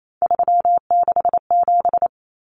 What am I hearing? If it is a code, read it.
Morse code, heard as a keyed tone, about 28 wpm, 700 Hz. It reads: 367
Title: Common Cut Numbers - 3-Digit 28wpm